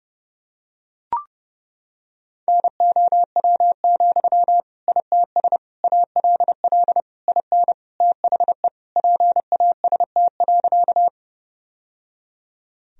Morse code, heard as a keyed tone, about 31 wpm, 700 Hz. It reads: NOW, ITS ALL IN THE PAST.
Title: Sentences from Top 1000 Words - Review Part 3 - 30wpm